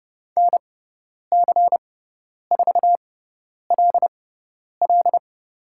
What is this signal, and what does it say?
Morse code, heard as a keyed tone, about 30 wpm, spaced as for 10 wpm, 700 Hz. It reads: NC4LL
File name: Lesson 16g - Letter c - Call Signs 30wpm